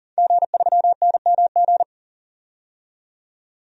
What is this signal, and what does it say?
Morse code, heard as a keyed tone, about 40 wpm, 700 Hz. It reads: G3NMG
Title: DXCC-100 Call Signs - 5 Characters - 40wpm